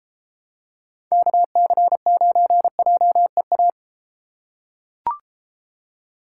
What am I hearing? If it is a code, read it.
Morse code, heard as a keyed tone, about 32 wpm, 700 Hz. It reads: KC9JEA